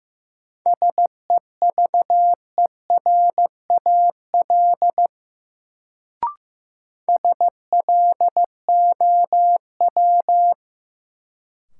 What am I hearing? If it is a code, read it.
Morse code, heard as a keyed tone, about 15 wpm, 700 Hz. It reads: SEVERAL SLOW
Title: Sentences from Top 400 Words - New Word Drill - 15wpm